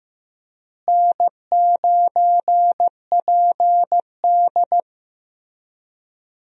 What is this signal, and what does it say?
Morse code, heard as a keyed tone, about 15 wpm, 700 Hz. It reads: N9PD
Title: USA Call Signs - 4 Characters - Repeated 3x - 15wpm